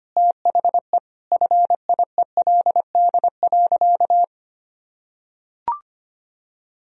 Morse code, heard as a keyed tone, about 25 wpm, 700 Hz. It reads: THE FIELD.
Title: Sentences from Top 1000 Words - Review Part 2 - 25wpm